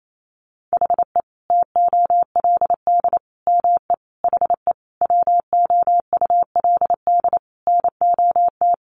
Morse code, heard as a keyed tone, about 28 wpm, 700 Hz. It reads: HE TOLD ME HE WOULD NOT
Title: Sentences from Top 1300 Words 28wpm